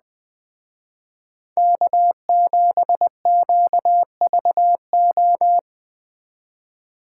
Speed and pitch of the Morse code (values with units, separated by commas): 20 wpm, 700 Hz